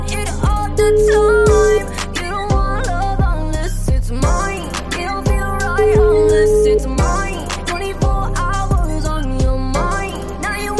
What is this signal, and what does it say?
Ringback tone (Europe, 425 Hz), with noise.